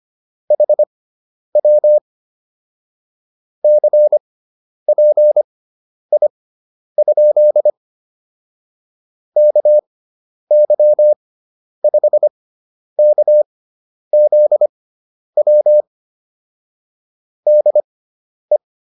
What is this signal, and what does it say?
Morse code, heard as a keyed tone, about 25 wpm, spaced as for 10 wpm, 600 Hz. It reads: HW CPI? KY5KZW DE
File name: QSO304 10